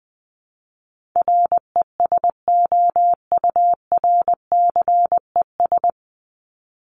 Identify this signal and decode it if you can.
Morse code, heard as a keyed tone, about 20 wpm, 700 Hz. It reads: RESOURCES
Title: ES:S Suffix - Top 100 Words with Base Words 20wpm